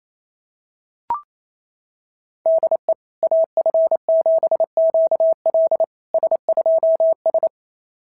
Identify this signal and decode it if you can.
Morse code, heard as a keyed tone, about 28 wpm, 690 Hz. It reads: DE AF7QL S2S
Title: SOTA QSO Practice 28wpm